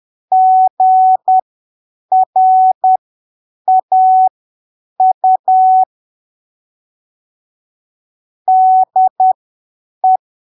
Morse code, heard as a keyed tone, about 10 wpm, spaced as for 7 wpm, 750 Hz. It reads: GRAU DE
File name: arquivos / texto / texto 11 10ppm